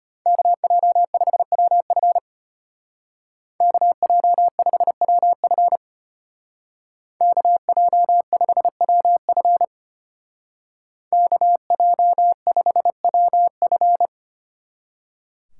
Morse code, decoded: KJ5WF KJ5WF KJ5WF KJ5WF

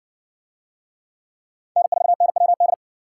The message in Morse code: N4NWD

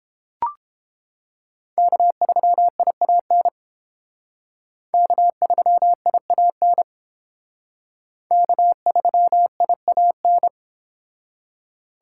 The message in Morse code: K3IAN K3IAN K3IAN

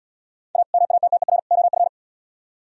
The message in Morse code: A MATTER OF

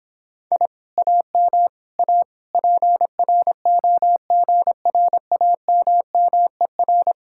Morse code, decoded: I AM A PROGRAMMER

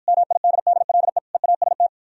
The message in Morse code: MIDDLE EAST

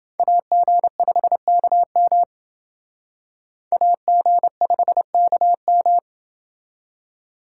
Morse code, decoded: AG5KM AG5KM

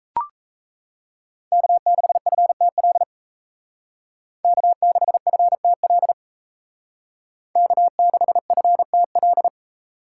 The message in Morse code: K6FTL K6FTL K6FTL

